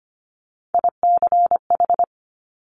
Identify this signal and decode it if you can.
Morse code, heard as a keyed tone, about 25 wpm, 700 Hz. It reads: ICH
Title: Top Three Letter Combinations in Words 25wpm